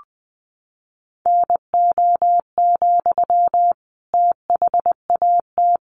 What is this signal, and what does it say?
Morse code, heard as a keyed tone, about 20 wpm, 700 Hz. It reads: NO, THAT
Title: Sentences from Top 300 Words 20wpm